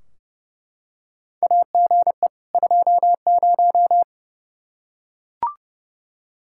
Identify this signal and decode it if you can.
Morse code, heard as a keyed tone, about 30 wpm, 700 Hz. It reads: AGE 20
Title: Practice QSO - Exchange 3 - More Info - 30wpm